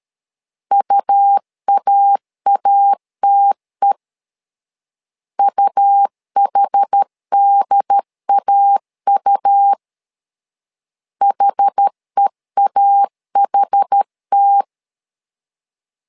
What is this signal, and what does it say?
Morse code, heard as a keyed tone, about 13 wpm, 800 Hz. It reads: UAATE UHDAU HEAHT
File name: LEZIONI / 2 UHDN / 2 UHD 12wpm 1540